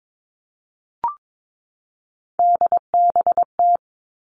Morse code, heard as a keyed tone, about 22 wpm, 700 Hz. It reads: DBT
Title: Three Challenging Letter-Number Combinations 22wpm